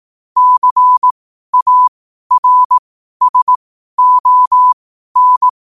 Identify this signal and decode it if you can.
Morse code, heard as a keyed tone, about 18 wpm, spaced as for 12 wpm, 1000 Hz. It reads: CARSON
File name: qotdfiles / 12wpm